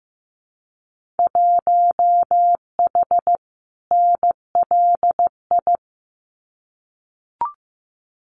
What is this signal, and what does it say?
Morse code, heard as a keyed tone, about 15 wpm, 700 Hz. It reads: 1H NLI
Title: Winter Field Day Exchanges 15wpm